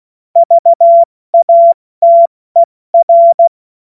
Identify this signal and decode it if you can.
Morse code, heard as a keyed tone, about 16 wpm, 680 Hz. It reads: VATER
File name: Morsekurs / Witze / Witz 10. August 2021, 70 F80